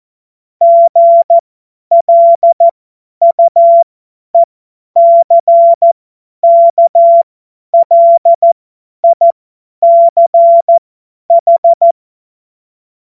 Morse code, heard as a keyed tone, about 14 wpm, spaced as for 10 wpm, 680 Hz. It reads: GLUECKLICH
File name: Morsekurs / Witze / Witz 20. Juni 2023 50 F70